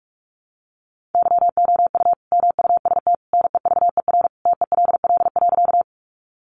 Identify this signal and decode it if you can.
Morse code, heard as a keyed tone, about 44 wpm, 700 Hz. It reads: YOU MUST NEVER TELL.
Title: Sentences from Top 300 Words 45wpm